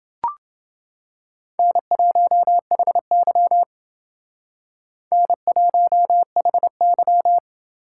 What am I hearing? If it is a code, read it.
Morse code, heard as a keyed tone, about 28 wpm, 700 Hz. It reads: N1HY N1HY